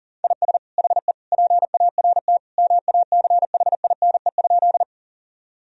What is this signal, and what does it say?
Morse code, heard as a keyed tone, about 40 wpm, 700 Hz. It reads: IS HE PART MACHINE?
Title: Sentences from Top 1000 Words - Review Part 2 - 40wpm